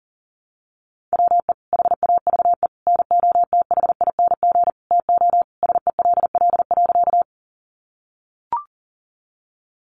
Morse code, heard as a keyed tone, about 40 wpm, 700 Hz. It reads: WE HAVE NOTHING TO SELL.
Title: Sentences from Top 1100 Words 40wpm